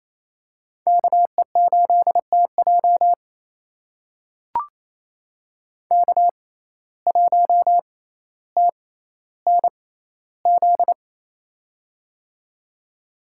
Morse code, heard as a keyed tone, about 28 wpm, spaced as for 14 wpm, 700 Hz. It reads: KE8TJ K1TNZ
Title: USA Call Signs - 5 Characters - 6x Character Spacing - 28wpm